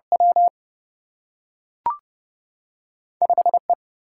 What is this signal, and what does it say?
Morse code, heard as a keyed tone, about 30 wpm, 700 Hz. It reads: W 5E
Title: Single-Letter Single-Number 30wpm